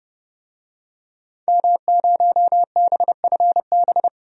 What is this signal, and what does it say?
Morse code, heard as a keyed tone, about 31 wpm, 700 Hz. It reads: M0BFB
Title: DXCC-100 Call Signs - 5 Characters - 30wpm